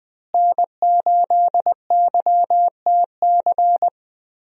Morse code, heard as a keyed tone, about 20 wpm, 700 Hz. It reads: N8YTC
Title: USA Call Signs - 5 Characters - Repeated 2x - 20wpm